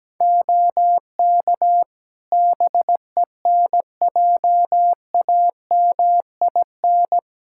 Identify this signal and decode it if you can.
Morse code, heard as a keyed tone, about 17 wpm, 700 Hz. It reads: OK BENJAMIN
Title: Practice QSO - Exchange 3 - More Info - 17wpm